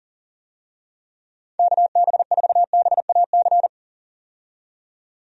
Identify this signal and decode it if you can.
Morse code, heard as a keyed tone, about 40 wpm, 700 Hz. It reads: KB4BAC
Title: USA Call Signs - 6 Characters - 40wpm